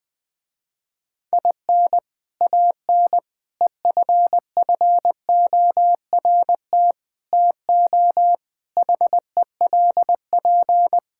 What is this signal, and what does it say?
Morse code, heard as a keyed tone, about 20 wpm, 700 Hz. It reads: IN AN EFFORT TO HELP
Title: Sentences with 4-Word Phrases - Spoken Beginning - 20wpm